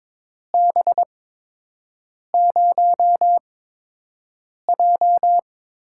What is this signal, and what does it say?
Morse code, heard as a keyed tone, about 22 wpm, spaced as for 6 wpm, 700 Hz. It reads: B0J
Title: USA Call Signs - 4 Characters - 8x Character Spacing - 22wpm